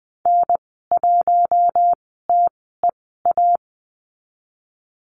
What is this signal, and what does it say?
Morse code, heard as a keyed tone, about 20 wpm, spaced as for 15 wpm, 700 Hz. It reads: N1TEA
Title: 082 Lesson 14g - Number 2 - Call Signs 20wpm